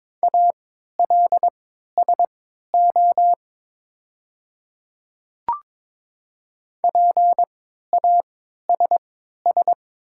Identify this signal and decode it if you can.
Morse code, heard as a keyed tone, about 22 wpm, spaced as for 12 wpm, 700 Hz. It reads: ALSO PASS
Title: 3x Character Spacing - Top 500 Words - 22wpm